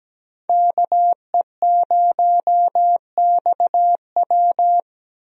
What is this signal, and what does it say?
Morse code, heard as a keyed tone, about 17 wpm, 700 Hz. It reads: KE0XW
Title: USA Call Signs - 5 Characters - Repeated 2x - 17wpm